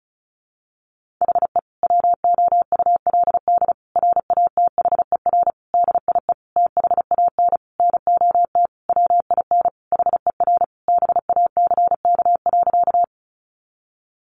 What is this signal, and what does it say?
Morse code, heard as a keyed tone, about 34 wpm, 700 Hz. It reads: HE WOULD RATHER DIE THAN NOT WIN HER BACK.
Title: Sentences from Top 1500 Words 35wpm